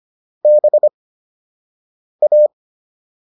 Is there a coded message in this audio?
Yes: Morse code from 0.5 to 2.5 s, heard as a keyed tone, about 25 wpm, spaced as for 6 wpm, 600 Hz. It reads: BA